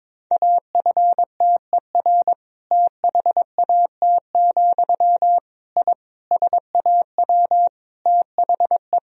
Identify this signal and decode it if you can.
Morse code, heard as a keyed tone, about 21 wpm, 700 Hz. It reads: AFTER THAT, I SAW THE
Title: Sentences from Top 1000 Words - Review Part 1 - 22wpm